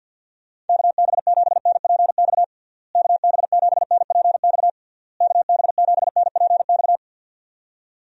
Morse code, heard as a keyed tone, about 50 wpm, 700 Hz. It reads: KB7NPX KB7NPX KB7NPX
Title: USA Call Signs - 6 Characters - Repeated 3x - 50wpm